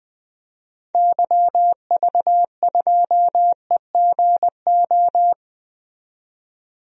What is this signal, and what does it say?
Morse code, heard as a keyed tone, about 20 wpm, 700 Hz. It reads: YV2EGO
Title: DXCC-100 Call Signs - 6 Characters - 20wpm